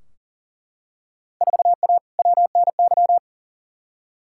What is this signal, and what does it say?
Morse code, heard as a keyed tone, about 40 wpm, 700 Hz. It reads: 4A WNY
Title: Field Day Exchanges Encore - 40wpm